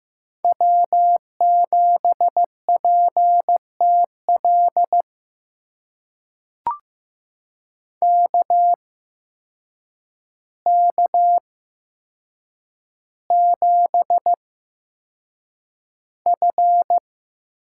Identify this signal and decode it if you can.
Morse code, heard as a keyed tone, about 15 wpm, spaced as for 7 wpm, 700 Hz. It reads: W7PTL KK7F
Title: USA Call Signs - 5 Characters - 8x Character Spacing - 15wpm